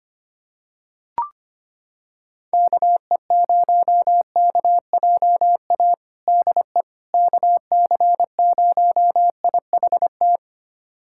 Morse code, heard as a keyed tone, about 25 wpm, 700 Hz. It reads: KE0KJA DE KC0IHT
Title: USA Callsign DE Callsign - 6 Characters - 25wpm